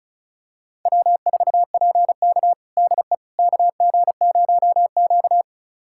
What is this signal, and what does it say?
Morse code, heard as a keyed tone, about 36 wpm, 700 Hz. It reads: W4PK DE KG0Q
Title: USA Callsign DE Callsign - 4 Characters - 35wpm